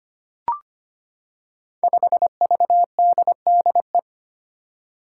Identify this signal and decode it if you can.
Morse code, heard as a keyed tone, about 25 wpm, 700 Hz. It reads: 5VDDE
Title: Five Challenging Letter-Number Combinations 25wpm